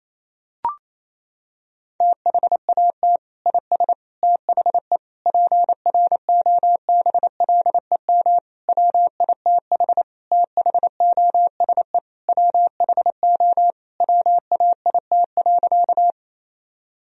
Morse code, heard as a keyed tone, about 28 wpm, 700 Hz. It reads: THAT IS THE PROBLEM WITH THOSE WHO WAIT.
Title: Sentences from Top 1000 Words - Review Part 2 - 28wpm